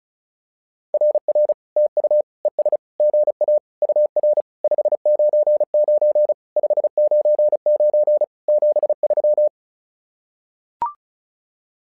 Morse code, heard as a keyed tone, about 36 wpm, 600 Hz. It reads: RR TU ES GA UR 599 599 73
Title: SOTA QSO Practice 35wpm